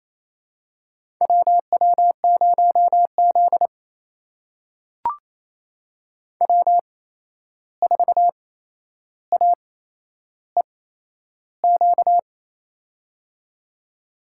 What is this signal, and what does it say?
Morse code, heard as a keyed tone, about 28 wpm, spaced as for 11 wpm, 700 Hz. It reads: WW0Z W4AEQ
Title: USA Call Signs - 8x Character Spacing - 28wpm